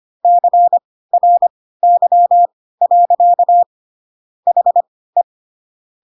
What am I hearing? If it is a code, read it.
Morse code, heard as a keyed tone, about 25 wpm, spaced as for 16 wpm, 700 Hz. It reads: CRY. HE
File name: Pinocchio / Pinocchio ch1 10 25 ik1hge